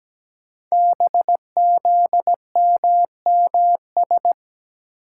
Morse code, heard as a keyed tone, about 17 wpm, 700 Hz. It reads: BZMMS
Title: Code Groups - 5 Letters - 17wpm